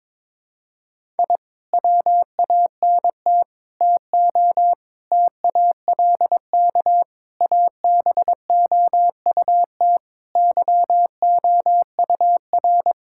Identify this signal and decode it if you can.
Morse code, heard as a keyed tone, about 22 wpm, 700 Hz. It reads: I WANT TO TALK ABOUT YOUR
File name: practice / 22wpm / Sentences with 4-Word Phrases - 22wpm